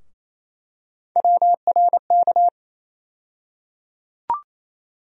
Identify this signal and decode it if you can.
Morse code, heard as a keyed tone, about 28 wpm, 700 Hz. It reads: WRK
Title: USA Call Sign Postfixes - 3 Letters 28wpm